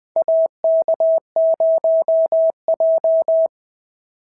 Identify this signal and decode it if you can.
Morse code, heard as a keyed tone, about 21 wpm, 690 Hz. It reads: AK0J
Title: SOTA QSO Practice 20wpm